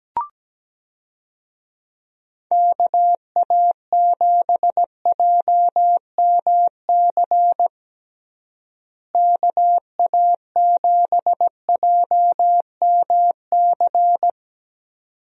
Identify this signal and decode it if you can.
Morse code, heard as a keyed tone, about 17 wpm, 700 Hz. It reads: KA7JMC KA7JMC
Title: USA Call Signs - 6 Characters - Repeated 2x - 17wpm